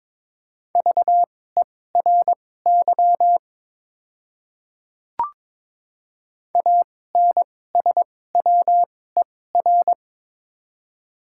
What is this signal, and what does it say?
Morse code, heard as a keyed tone, about 22 wpm, spaced as for 16 wpm, 690 Hz. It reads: VERY ANSWER
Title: 2x Character Spacing - Top 300 Words - 22wpm